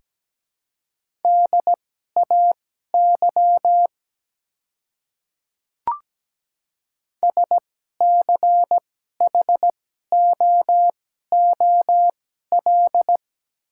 Morse code, heard as a keyed tone, about 17 wpm, spaced as for 12 wpm, 700 Hz. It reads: DAY SCHOOL
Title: 2x Character Spacing - Top 300 Words - 17wpm